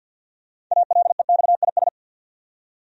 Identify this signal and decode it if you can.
Morse code, heard as a keyed tone, about 53 wpm, 700 Hz. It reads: ALEXIS
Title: USA Common Names - 50wpm